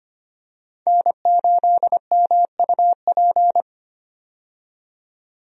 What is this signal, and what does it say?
Morse code, heard as a keyed tone, about 25 wpm, 700 Hz. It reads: N8MUP
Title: USA Call Signs - 5 Characters - 25wpm